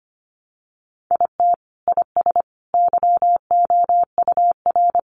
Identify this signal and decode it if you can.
Morse code, heard as a keyed tone, about 25 wpm, 700 Hz. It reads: IT IS YOUR
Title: Sentences from Top 600 Words 25wpm